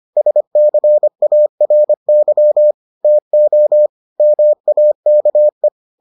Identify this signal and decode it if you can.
Morse code, heard as a keyed tone, about 25 wpm, 600 Hz. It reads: SCARY TO MAKE